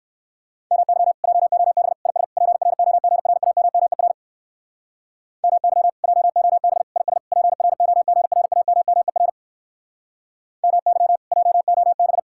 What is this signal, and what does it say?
Morse code, heard as a keyed tone, about 56 wpm, 700 Hz. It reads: MY JOB IS PROGRAMMER MY JOB IS PROGRAMMER MY JOB
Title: Speed-Racing Common QSO Phrases - 45wpm